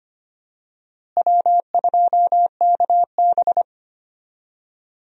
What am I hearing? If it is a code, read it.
Morse code, heard as a keyed tone, about 25 wpm, 700 Hz. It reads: W2KB
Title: USA Call Signs - 4 Characters - Repeated 2x - 25wpm